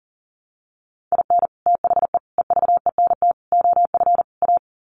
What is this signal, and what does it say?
Morse code, heard as a keyed tone, about 40 wpm, 700 Hz. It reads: IN THE EVENT OF A
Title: Sets of 5 Words Encore 40wpm